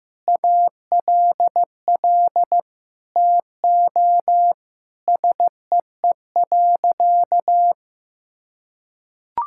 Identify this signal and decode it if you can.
Morse code, heard as a keyed tone, about 15 wpm, 700 Hz. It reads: ALL TO SEE.
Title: Sentences from Top 1000 Words - Review Part 5 - 15wpm